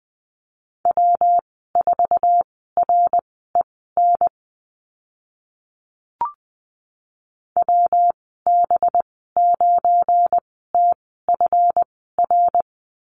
Morse code, heard as a keyed tone, about 20 wpm, spaced as for 14 wpm, 700 Hz. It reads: W4REN WB9TFR
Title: 193 Lesson 31g - Letter v - Call Signs 20wpm